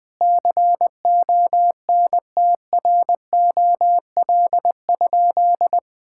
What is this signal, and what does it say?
Morse code, heard as a keyed tone, about 20 wpm, 700 Hz. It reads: CONTROL?
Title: Sentences from Top 1000 Words - Review Part 5 - 20wpm